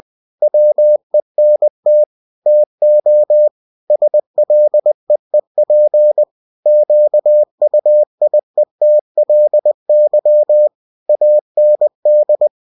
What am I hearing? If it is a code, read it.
Morse code, heard as a keyed tone, about 20 wpm, 600 Hz. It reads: WENT TO SLEEP QUIETLY AND
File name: cw-books / winnie-the-pooh / winnie-20wpm-page-0101